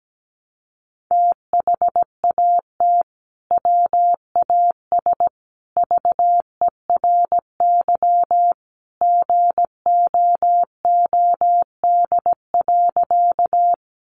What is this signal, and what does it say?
Morse code, heard as a keyed tone, about 17 wpm, 700 Hz. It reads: THAT WAS VERY GOOD.
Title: Sentences from Top 200 Words 17wpm